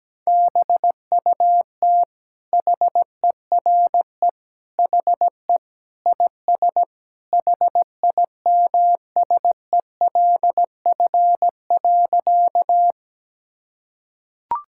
Morse code, heard as a keyed tone, about 17 wpm, 700 Hz. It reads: BUT HERE HE IS HIMSELF.
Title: Sentences from Top 1600 Words 17wpm